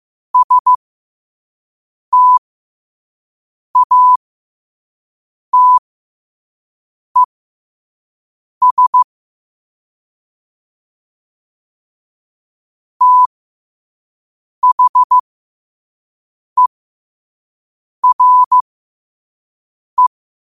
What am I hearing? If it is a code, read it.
Morse code, heard as a keyed tone, about 15 wpm, spaced as for 5 wpm, 1000 Hz. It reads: STATES THERE
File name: qotdfiles / 5wpm